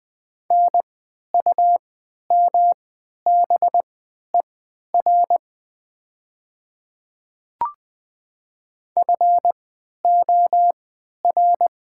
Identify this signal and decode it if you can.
Morse code, heard as a keyed tone, about 21 wpm, spaced as for 11 wpm, 700 Hz. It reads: NUMBER FOR
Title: 3x Character Spacing - Top 300 Words - 20wpm